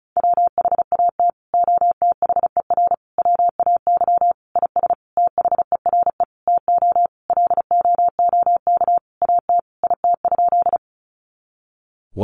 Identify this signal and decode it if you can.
Morse code, heard as a keyed tone, about 34 wpm, 700 Hz. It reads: WHAT OTHER WAY IS THERE TO LOOK AT IT?
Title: Sentences from Top 100 Words 35wpm